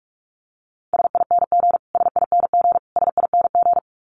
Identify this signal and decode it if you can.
Morse code, heard as a keyed tone, about 44 wpm, 700 Hz. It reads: SING SING SING
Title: Top 500 Words Repeated 45wpm